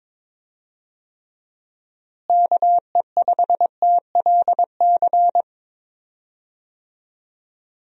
Morse code, heard as a keyed tone, about 22 wpm, 700 Hz. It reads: KE5TLC
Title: USA Call Signs - 6 Characters - Repeated 3x - 22wpm